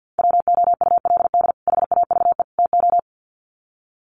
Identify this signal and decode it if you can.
Morse code, heard as a keyed tone, about 50 wpm, 700 Hz. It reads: WOULD HAVE TO
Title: Sets of 3 Words 50wpm